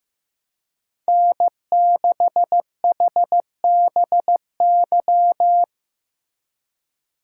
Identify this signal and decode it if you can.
Morse code, heard as a keyed tone, about 15 wpm, 700 Hz. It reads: N6HBY
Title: USA Call Signs - 4x Character Spacing - 15wpm